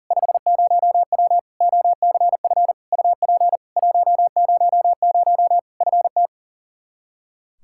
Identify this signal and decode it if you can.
Morse code, heard as a keyed tone, about 37 wpm, 700 Hz. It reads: 50W OCF UP 100 FT
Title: Practice QSO - Exchange 2 - Info - 40wpm